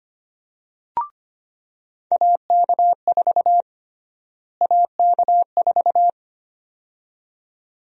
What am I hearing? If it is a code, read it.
Morse code, heard as a keyed tone, about 25 wpm, 700 Hz. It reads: AK4 AK4
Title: USA Call Sign Prefixes - Repeated 2x - 25wpm